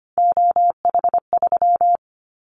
Morse code, heard as a keyed tone, about 25 wpm, 700 Hz. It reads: OH3
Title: Callsign Prefixes for Top 50 DXCC - 25wpm